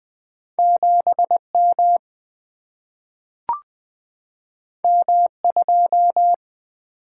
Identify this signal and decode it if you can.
Morse code, heard as a keyed tone, about 20 wpm, 700 Hz. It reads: 7 M M 2
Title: Single-Letter Single-Number 20wpm